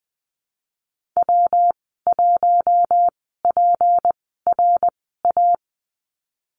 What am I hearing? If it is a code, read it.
Morse code, heard as a keyed tone, about 19 wpm, spaced as for 14 wpm, 700 Hz. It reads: W1PRA
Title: 154 Lesson 25g - Letter p - Call Signs 20wpm